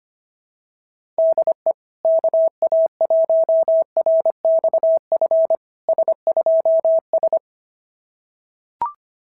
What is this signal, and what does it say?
Morse code, heard as a keyed tone, about 25 wpm, 690 Hz. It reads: DE KA1RXF S2S
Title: SOTA QSO Practice 25wpm